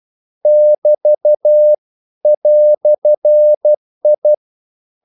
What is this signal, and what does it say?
Morse code, heard as a keyed tone, about 12 wpm, 600 Hz. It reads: <BT> "I
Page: English quotations in CW (telegraphy - morse)